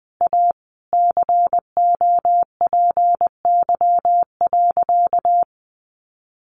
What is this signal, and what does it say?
Morse code, heard as a keyed tone, about 19 wpm, 700 Hz. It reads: A COPY.
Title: Sentences from Top 1400 Words 20wpm